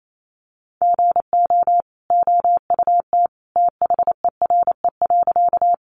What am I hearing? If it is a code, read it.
Morse code, heard as a keyed tone, about 28 wpm, 700 Hz. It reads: GO OUT THERE.
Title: A Sound Of Thunder 28wpm